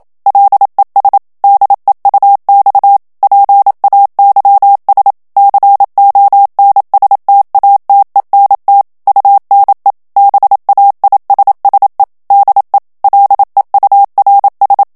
Morse code, heard as a keyed tone, about 27 wpm, 800 Hz. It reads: LES DEUX PAYS CONSTATENT UNE BAISSE DE LEURS
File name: CW / FICHIERS-MP3 / FRANCAIS TEXTE CLAIR 1 VITESSE 26 MOTS MINUTES